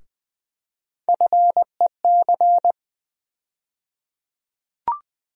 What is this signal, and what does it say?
Morse code, heard as a keyed tone, about 20 wpm, 700 Hz. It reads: FEC
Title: USA Call Sign Postfixes - 3 Letters 20wpm